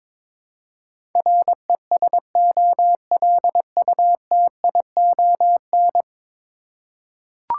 Morse code, heard as a keyed tone, about 22 wpm, 700 Hz. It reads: RESOLUTION
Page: morse-code-ninja-practice-22wpm